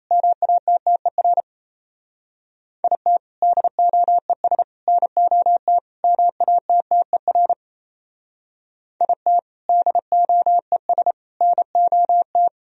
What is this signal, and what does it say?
Morse code, heard as a keyed tone, about 35 wpm, 700 Hz. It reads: MATTER IT DOES NOT MATTER IT DOES NOT
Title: Speed-Racing - Sets of 4 Words 28wpm